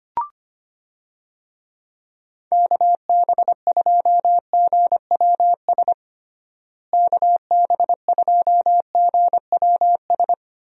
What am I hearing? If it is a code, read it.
Morse code, heard as a keyed tone, about 25 wpm, 700 Hz. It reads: KB2GWS KB2GWS
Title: USA Call Signs - 6 Characters - Repeated 2x - 25wpm